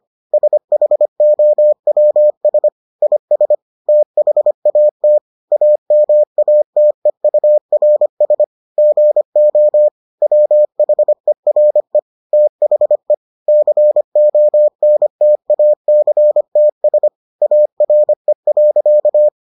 Morse code, heard as a keyed tone, about 25 wpm, 600 Hz. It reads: SHOWS IS THAT AMATEURS GO WHERE THE CONTACTS ARE.